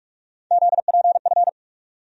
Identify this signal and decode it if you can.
Morse code, heard as a keyed tone, about 42 wpm, 700 Hz. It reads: ZPF
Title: USA Call Sign Postfixes - 3 Letters 45wpm